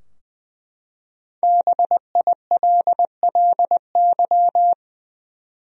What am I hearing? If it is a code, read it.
Morse code, heard as a keyed tone, about 20 wpm, 700 Hz. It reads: BILLY
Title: USA Common Names - Max 5 Letters - 20wpm